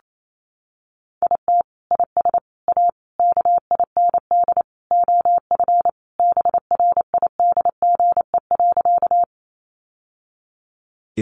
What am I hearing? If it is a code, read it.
Morse code, heard as a keyed tone, about 28 wpm, 700 Hz. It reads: IT IS A KIND OF BRIDGE.
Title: Sentences from Top 1200 Words 28wpm